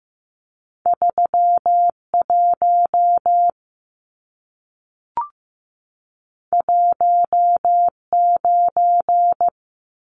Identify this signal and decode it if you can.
Morse code, heard as a keyed tone, about 15 wpm, 700 Hz. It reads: 31 19
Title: Numbers - 2-Digit 15wpm